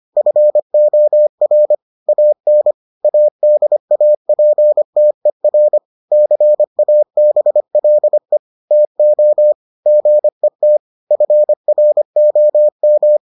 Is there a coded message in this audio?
Yes: Morse code from 0.2 to 13.2 s, heard as a keyed tone, about 25 wpm, 600 Hz. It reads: FOR AN ADAPTER CABLE TO GET FROM